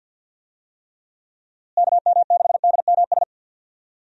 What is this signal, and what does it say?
Morse code, heard as a keyed tone, about 52 wpm, 700 Hz. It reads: KM6DMS
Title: USA Call Signs - 6 Characters - Repeated 3x - 50wpm